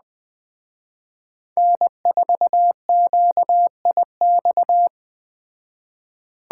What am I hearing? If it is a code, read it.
Morse code, heard as a keyed tone, about 21 wpm, 700 Hz. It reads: N4QIX